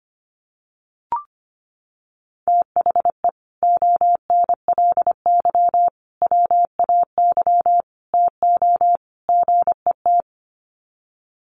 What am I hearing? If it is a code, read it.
Morse code, heard as a keyed tone, about 25 wpm, 700 Hz. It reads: THE ONLY WAY TO GET
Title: Sets of 5 Words Encore2 25wpm